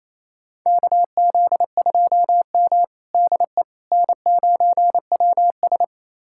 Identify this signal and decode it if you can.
Morse code, heard as a keyed tone, about 28 wpm, 700 Hz. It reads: KZ2M DE N9WS
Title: USA Callsign DE Callsign - 4 Characters - 28wpm